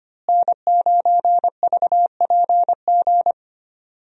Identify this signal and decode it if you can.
Morse code, heard as a keyed tone, about 25 wpm, 700 Hz. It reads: N9VPG
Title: USA Call Signs - 5 Characters - Repeated 2x - 25wpm